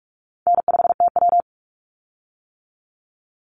Morse code, heard as a keyed tone, about 46 wpm, 700 Hz. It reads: N5TW
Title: Easy Call Signs - 45wpm